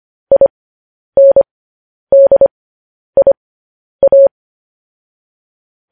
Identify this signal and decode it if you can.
Morse code, heard as a keyed tone, about 25 wpm, spaced as for 10 wpm, 550 Hz. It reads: INDIA
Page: news headlines in morse code at 10 wpm